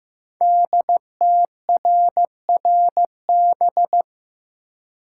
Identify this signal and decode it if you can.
Morse code, heard as a keyed tone, about 15 wpm, 700 Hz. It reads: DTRRB
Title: Code Groups - 5 Letters - 15wpm